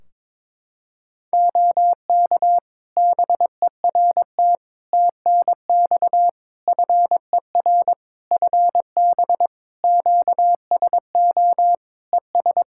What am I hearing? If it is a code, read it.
Morse code, heard as a keyed tone, about 22 wpm, 700 Hz. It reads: OK BERT TNX FER FB QSO ES
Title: Practice QSO - Exchange 4 - Ending - 22wpm